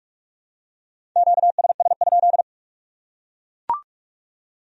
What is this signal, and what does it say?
Morse code, heard as a keyed tone, about 44 wpm, 700 Hz. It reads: QSS?